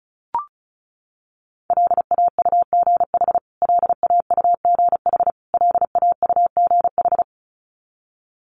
Morse code, heard as a keyed tone, about 34 wpm, 700 Hz. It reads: LAUGH LAUGH LAUGH
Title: Top 500 Words Repeated 35wpm